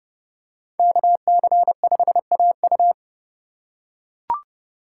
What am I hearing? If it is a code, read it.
Morse code, heard as a keyed tone, about 30 wpm, 700 Hz. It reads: KC5AU
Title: USA Call Signs - 5 Characters - 3x Character Spacing - 30wpm